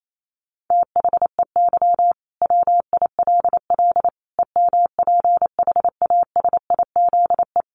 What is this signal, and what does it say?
Morse code, heard as a keyed tone, about 28 wpm, 700 Hz. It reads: THEY WILL EMPHASIZE
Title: Sentences from Top 1500 Words 28wpm